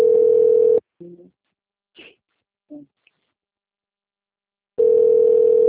Ringback tone (North America), 440 + 480 Hz.